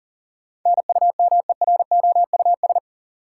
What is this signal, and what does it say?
Morse code, heard as a keyed tone, about 40 wpm, 700 Hz. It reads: NUMEROUS
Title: Sentences from Top 1400 Words - New Word Drill - 40wpm